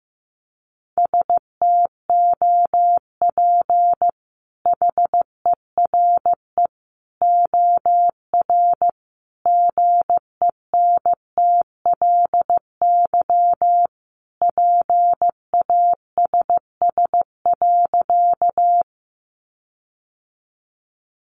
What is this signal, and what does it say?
Morse code, heard as a keyed tone, about 15 wpm, 700 Hz. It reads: STOP HERE OR GENTLY PASS.
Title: Sentences from Top 1500 Words 15wpm